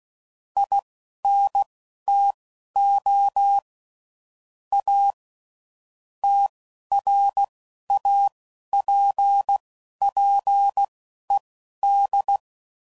Morse code, heard as a keyed tone, about 16 wpm, spaced as for 11 wpm, 800 Hz. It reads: INTO A TRAPPED
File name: CW Code Groups @10WPM / QSOs / QSO kc7haa